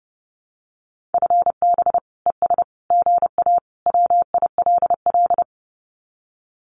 Morse code, heard as a keyed tone, about 30 wpm, 700 Hz. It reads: FB ES GA WILL
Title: Common QSO Phrases Encore 30wpm